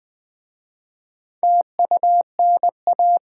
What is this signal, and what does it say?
Morse code, heard as a keyed tone, about 20 wpm, 700 Hz. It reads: TUNA
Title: Difficult Words with Hints - 20wpm